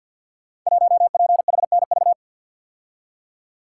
Morse code, heard as a keyed tone, about 48 wpm, 700 Hz. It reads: 1PHNV
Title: Code Groups - 5 Letter:Numbers - 50wpm